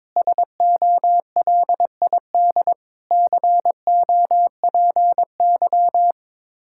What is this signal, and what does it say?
Morse code, heard as a keyed tone, about 22 wpm, 700 Hz. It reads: SOLID COPY
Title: K1USN SST QSO Practice 22wpm